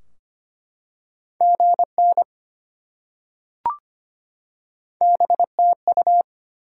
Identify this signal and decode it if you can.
Morse code, heard as a keyed tone, about 25 wpm, 700 Hz. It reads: GN BTU
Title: Top 180 QSO Elements 25wpm